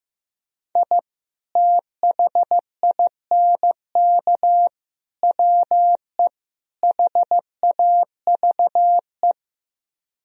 Morse code, heard as a keyed tone, about 15 wpm, 700 Hz. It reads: I THINK WE HAVE
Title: Speed-Racing - Sets of 4 Words 15wpm